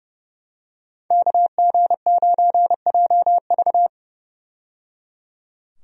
Morse code, heard as a keyed tone, about 30 wpm, 700 Hz. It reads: KG9JV
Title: USA Call Signs - 5 Characters - 30wpm